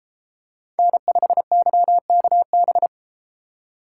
Morse code, heard as a keyed tone, about 32 wpm, 700 Hz. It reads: N5YKB